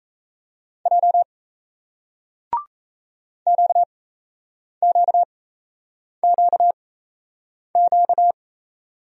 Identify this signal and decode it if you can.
Morse code, heard as a keyed tone, about 38 wpm, spaced as for 8 wpm, 700 Hz. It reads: J QQQQ